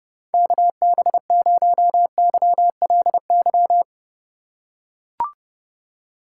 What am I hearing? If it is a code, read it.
Morse code, heard as a keyed tone, about 31 wpm, 700 Hz. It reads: KB0YLY